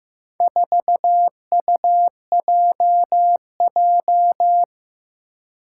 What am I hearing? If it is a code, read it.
Morse code, heard as a keyed tone, about 15 wpm, 700 Hz. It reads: 4UJJ
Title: Four Challenging Letter-Number Combinations 15wpm